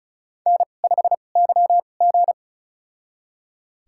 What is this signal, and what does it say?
Morse code, heard as a keyed tone, about 36 wpm, spaced as for 25 wpm, 700 Hz. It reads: N5YG
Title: USA Call Signs - 4 Characters - 2x Character Spacing - 35wpm